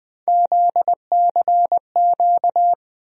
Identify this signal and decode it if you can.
Morse code, heard as a keyed tone, about 20 wpm, 700 Hz. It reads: ZCQ
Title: USA Call Sign Postfixes - 3 Letters 20wpm